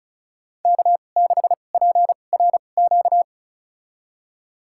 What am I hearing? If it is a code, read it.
Morse code, heard as a keyed tone, about 34 wpm, spaced as for 25 wpm, 700 Hz. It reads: K6PRQ
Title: USA Call Signs - 5 Characters - 2x Character Spacing - 35wpm